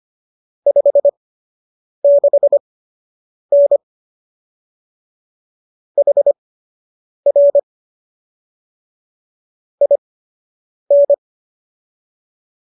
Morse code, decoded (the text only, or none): 56N HR IN